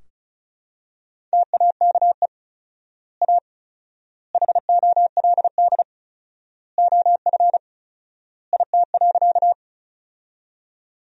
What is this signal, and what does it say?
Morse code, heard as a keyed tone, about 36 wpm, 700 Hz. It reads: TAKE A HOLD OF IT.
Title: 4x Word-Spacing - Sentences from Top 400 Words - 35wpm